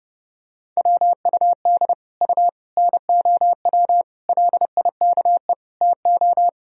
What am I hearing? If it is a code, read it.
Morse code, heard as a keyed tone, about 30 wpm, 700 Hz. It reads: WUD U NOW LIKE TO
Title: Sentences from Top 200 Words and Abbreviations 30wpm